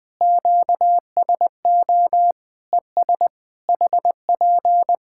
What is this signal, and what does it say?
Morse code, heard as a keyed tone, about 20 wpm, 700 Hz. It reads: QSO ES HP
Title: Practice QSO - Exchange 4 - Ending - 20wpm